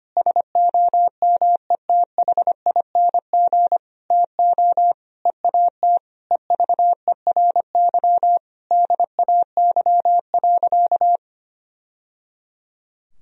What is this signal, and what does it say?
Morse code, heard as a keyed tone, about 25 wpm, 700 Hz. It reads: SOMETHING TO EAT EVERY DAY.
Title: Sentences from Top 1000 Words - Review Part 4 - 25wpm